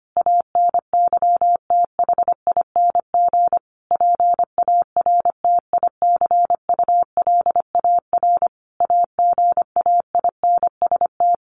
Morse code, heard as a keyed tone, about 25 wpm, 700 Hz. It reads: ANYTHING PARTICULAR AGAINST
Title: Sentences from Top 1000 Words 25wpm